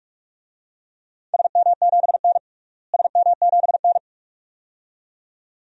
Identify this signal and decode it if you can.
Morse code, heard as a keyed tone, about 46 wpm, 700 Hz. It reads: SM7N SM7N
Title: DXCC-100 Call Signs - 4 Characters - Repeated 2x - 45wpm